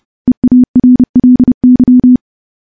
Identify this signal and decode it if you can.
Morse code, heard as a keyed tone, about 29 wpm, 250 Hz. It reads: EARLY
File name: morse / morse-training-30WPM